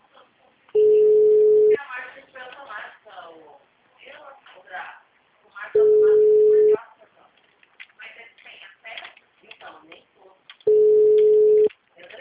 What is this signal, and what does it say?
Ringback tone (Europe, 425 Hz), with speech.
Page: Limeira - SP